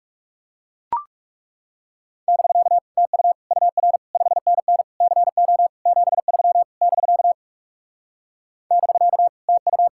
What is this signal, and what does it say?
Morse code, heard as a keyed tone, about 40 wpm, 700 Hz. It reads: <BK> TU UR 5NN CO 73 <BK> <BK> TU